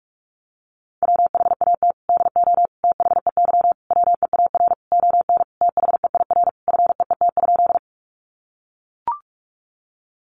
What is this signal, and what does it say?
Morse code, heard as a keyed tone, about 44 wpm, 700 Hz. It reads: WHAT DO THEY WEAR ON THEIR FEET?
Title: Sentences from Top 1500 Words 45wpm